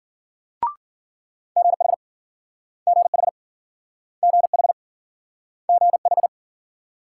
Morse code, heard as a keyed tone, about 55 wpm, 700 Hz. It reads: GH GH GH GH